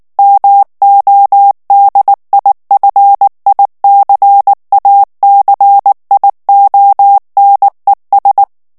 Morse code, heard as a keyed tone, about 19 wpm, 800 Hz. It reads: MODIFICACIONES
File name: morse mp3 / ESPAGNOL TEXTE EN CLAIR 2 VITESSE 18 MOTS MINUTES